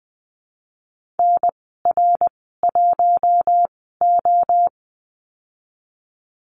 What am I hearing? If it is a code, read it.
Morse code, heard as a keyed tone, about 20 wpm, spaced as for 14 wpm, 700 Hz. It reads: NR1O
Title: 062 Lesson 11g - Letter h - Call Signs 20wpm